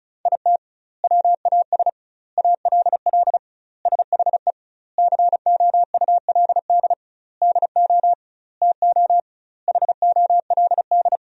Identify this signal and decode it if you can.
Morse code, heard as a keyed tone, about 36 wpm, 700 Hz. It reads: IT WAS ALL SHE COULD DO TO HOLD
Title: 2x Word-Spacing - Sentences from Top 400 Words - 35wpm